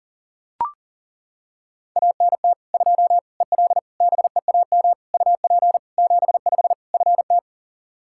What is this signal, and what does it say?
Morse code, heard as a keyed tone, about 40 wpm, 700 Hz. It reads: ANT 2 EL BEAM UP 75 FT
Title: Practice QSO - Exchange 2 - Info - 40wpm